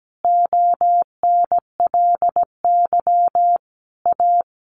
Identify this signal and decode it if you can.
Morse code, heard as a keyed tone, about 17 wpm, 700 Hz. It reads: ONLY A
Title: Sentences from Top 1500 Words 17wpm